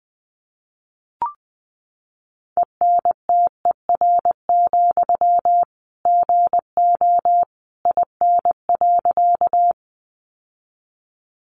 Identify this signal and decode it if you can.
Morse code, heard as a keyed tone, about 20 wpm, 700 Hz. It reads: ENTER, GO IN.
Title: Sentences from Top 1300 Words 20wpm